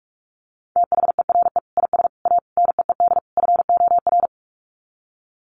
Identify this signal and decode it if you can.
Morse code, heard as a keyed tone, about 44 wpm, 700 Hz. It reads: THERE IS A NEED FOR
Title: Sets of 5 Words Encore2 45wpm